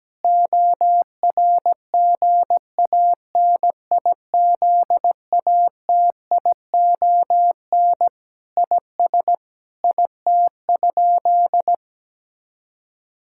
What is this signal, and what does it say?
Morse code, heard as a keyed tone, about 17 wpm, 700 Hz. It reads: ORGANIZATION IS IT?
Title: Sentences from Top 1000 Words - Review Part 5 - 17wpm